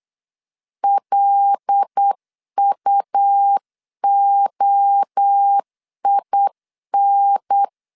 Morse code, heard as a keyed tone, about 8 wpm, 800 Hz. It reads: LUOIN
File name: CW / DET / 1 6 1500C S C 8wpm 5219